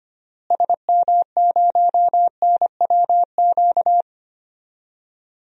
Morse code, heard as a keyed tone, about 25 wpm, 700 Hz. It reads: SM0NWQ
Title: DXCC-100 Call Signs - 6 Characters - Repeated 3x - 25wpm